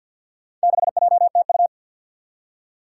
Morse code, heard as a keyed tone, about 52 wpm, 700 Hz. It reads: 6JTU
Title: Four Challenging Letter-Number Combinations 50wpm